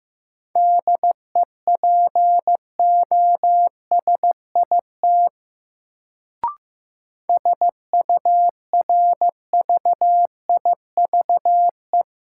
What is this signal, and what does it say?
Morse code, heard as a keyed tone, about 15 wpm, 700 Hz. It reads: DEPOSIT SURVIVE
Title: Sentences from Top 1300 Words - New Word Drill Alt - 15wpm